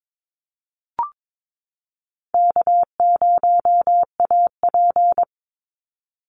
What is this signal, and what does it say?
Morse code, heard as a keyed tone, about 22 wpm, 700 Hz. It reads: K0AP
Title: Call Signs - 22wpm